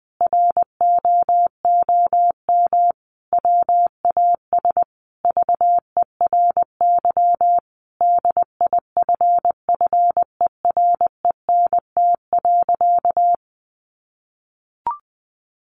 Morse code, heard as a keyed tone, about 20 wpm, 700 Hz. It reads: ROOM WAS VERY DIFFERENT.
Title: Sentences from Top 1300 Words 20wpm